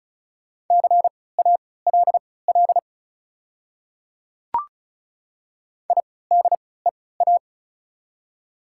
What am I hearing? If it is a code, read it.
Morse code, heard as a keyed tone, about 36 wpm, spaced as for 20 wpm, 690 Hz. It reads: CALL IDEA
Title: 3x Character Spacing - Top 300 Words - 35wpm